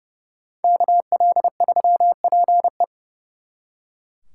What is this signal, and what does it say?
Morse code, heard as a keyed tone, about 35 wpm, 700 Hz. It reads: KL3PE
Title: Speed Racing - USA 2x2 Call Signs 30wpm